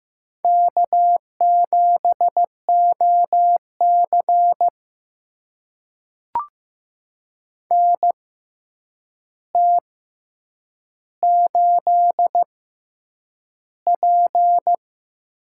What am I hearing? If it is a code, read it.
Morse code, heard as a keyed tone, about 15 wpm, spaced as for 8 wpm, 700 Hz. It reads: K7OC NT8P
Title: USA Call Signs - 4 Characters - 6x Character Spacing - 15wpm